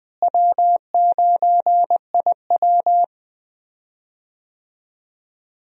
Morse code, heard as a keyed tone, about 20 wpm, 700 Hz. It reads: W9IW
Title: USA Call Signs - 4 Characters - Repeated 2x - 20wpm